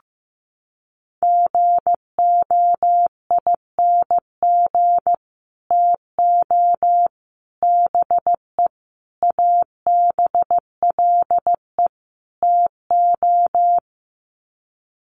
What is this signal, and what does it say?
Morse code, heard as a keyed tone, about 15 wpm, 700 Hz. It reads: GOING TO BE ABLE TO
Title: Sets of 5 Words 15wpm